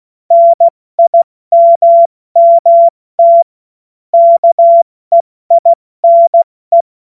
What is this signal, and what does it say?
Morse code, heard as a keyed tone, about 16 wpm, 680 Hz. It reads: NIMMT KEINE
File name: Morsekurs / Witze / Witz 15. November 2022 70 F80